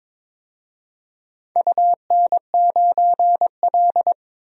Morse code, heard as a keyed tone, about 21 wpm, 700 Hz. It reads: UN9L
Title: DXCC-100 Call Signs - 4 Characters - 22wpm